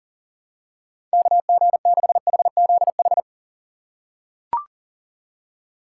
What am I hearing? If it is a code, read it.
Morse code, heard as a keyed tone, about 37 wpm, 700 Hz. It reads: KG6HZH
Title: USA Call Signs - 6 Characters - 4x Character Spacing - 40wpm